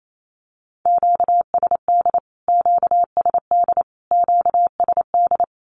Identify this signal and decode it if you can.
Morse code, heard as a keyed tone, about 28 wpm, 700 Hz. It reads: QSD QSD QSD
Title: Q-Codes Repeated 3x 28wpm